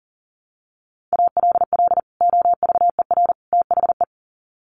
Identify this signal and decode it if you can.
Morse code, heard as a keyed tone, about 40 wpm, 700 Hz. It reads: ALL OVER THE
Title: Sets of 3 Words 40wpm